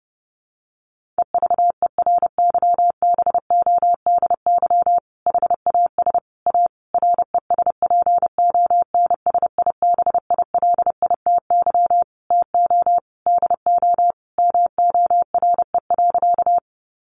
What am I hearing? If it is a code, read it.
Morse code, heard as a keyed tone, about 30 wpm, 700 Hz. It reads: EVERYBODY HAS A RESPONSIBILITY TO DO MORE.
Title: Sentences from Top 1400 Words 30wpm